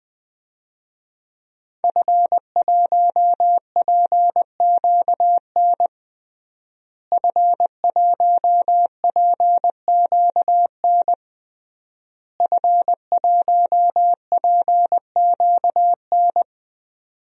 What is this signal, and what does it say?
Morse code, heard as a keyed tone, about 20 wpm, 700 Hz. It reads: F1PQN F1PQN F1PQN
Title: DXCC-100 Call Signs - 5 Characters - Repeated 3x - 20wpm